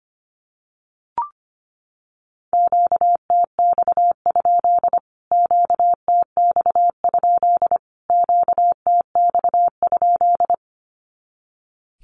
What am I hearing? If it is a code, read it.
Morse code, heard as a keyed tone, about 25 wpm, 700 Hz. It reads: QTX? QTX? QTX?
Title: Q-Codes Repeated 3x 25wpm